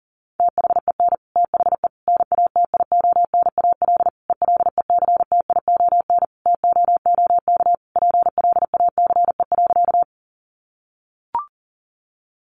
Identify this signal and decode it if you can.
Morse code, heard as a keyed tone, about 40 wpm, 700 Hz. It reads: THEN THE NATIONAL ELECTION TOOK PLACE.
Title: Sentences from Top 1500 Words 40wpm